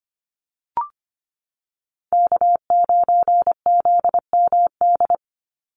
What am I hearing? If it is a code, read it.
Morse code, heard as a keyed tone, about 25 wpm, 700 Hz. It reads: K9ZMD
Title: NA SOTA Chaser Callsigns 25wpm